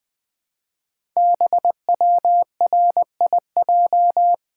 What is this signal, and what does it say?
Morse code, heard as a keyed tone, about 20 wpm, 700 Hz. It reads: BWRIJ
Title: Code Groups - 5 Letters - 20wpm